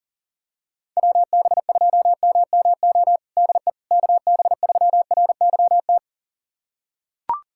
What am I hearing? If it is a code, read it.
Morse code, heard as a keyed tone, about 40 wpm, 700 Hz. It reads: WB2MMO DE KB3RYT
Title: USA Callsign DE Callsign - 6 Characters - 40wpm